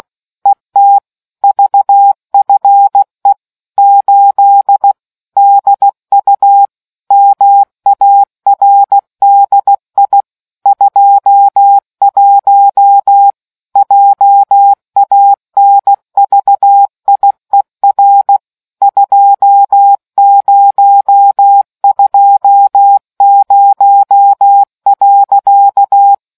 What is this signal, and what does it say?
Morse code, heard as a keyed tone, about 16 wpm, 800 Hz. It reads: ET VFE 8 DU MARDI 21 JANVIER 2020.